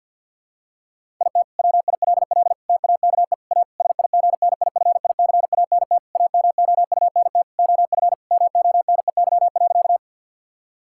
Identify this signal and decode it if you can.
Morse code, heard as a keyed tone, about 50 wpm, 700 Hz. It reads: IT WILL TAKE A SIGNIFICANT AMOUNT OF MONEY.
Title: Sentences from Top 1000 Words - Review Part 6 - 50wpm